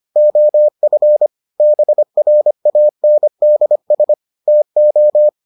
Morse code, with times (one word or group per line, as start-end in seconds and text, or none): 0.2-1.3 OF
1.6-4.1 BRANDS
4.5-5.3 TO